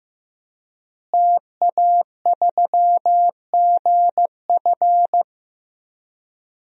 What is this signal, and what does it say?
Morse code, heard as a keyed tone, about 15 wpm, 700 Hz. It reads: TA3GF
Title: DXCC-100 Call Signs - 5 Characters - Repeated 3x - 15wpm